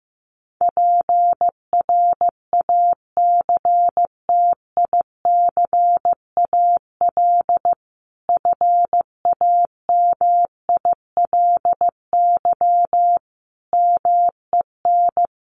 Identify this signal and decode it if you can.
Morse code, heard as a keyed tone, about 15 wpm, 700 Hz. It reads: PRACTICAL FAMILY MEN
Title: Sentences from Top 1400 Words 15wpm